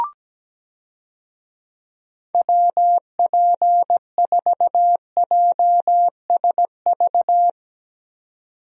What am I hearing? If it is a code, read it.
Morse code, heard as a keyed tone, about 17 wpm, 700 Hz. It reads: WP4JSV
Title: USA Call Signs - 6 Characters - 17wpm